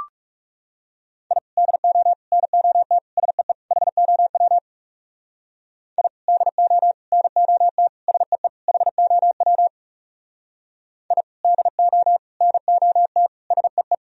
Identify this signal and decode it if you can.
Morse code, heard as a keyed tone, about 38 wpm, 700 Hz. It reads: I DO NOT SEE HOW I DO NOT SEE HOW I DO NOT SEE